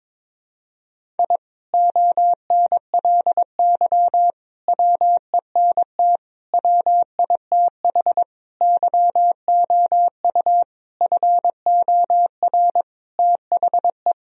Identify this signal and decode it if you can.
Morse code, heard as a keyed tone, about 22 wpm, 700 Hz. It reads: I ONLY WENT WITH YOU FOR THE
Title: Sentences from Top 1000 Words - Review Part 1 - 22wpm